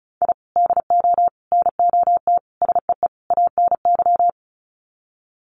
Morse code, heard as a keyed tone, about 34 wpm, 700 Hz. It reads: I DO NOT SEE ANY
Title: Sets of 5 Words 35wpm